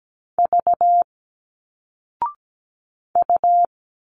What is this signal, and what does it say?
Morse code, heard as a keyed tone, about 17 wpm, 700 Hz. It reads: V U
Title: Single Letters 17wpm